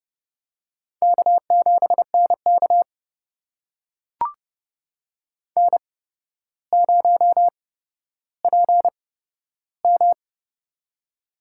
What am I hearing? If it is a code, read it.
Morse code, heard as a keyed tone, about 30 wpm, spaced as for 13 wpm, 700 Hz. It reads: K7NK N0PM
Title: USA Call Signs - 4 Characters - 8x Character Spacing - 30wpm